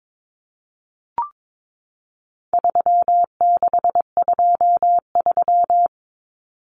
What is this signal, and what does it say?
Morse code, heard as a keyed tone, about 22 wpm, 700 Hz. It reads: 3623
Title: Numbers - 4-Digit 22wpm